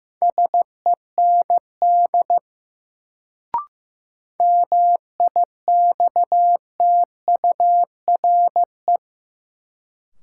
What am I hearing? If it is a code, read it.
Morse code, heard as a keyed tone, about 15 wpm, 700 Hz. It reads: SEND MIXTURE
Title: Sentences from Top 1500 Words - New Word Drill - 15wpm